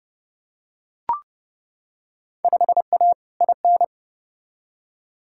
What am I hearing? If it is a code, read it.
Morse code, heard as a keyed tone, about 29 wpm, 700 Hz. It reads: 5A IN
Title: Field Day Exchanges Encore - 30wpm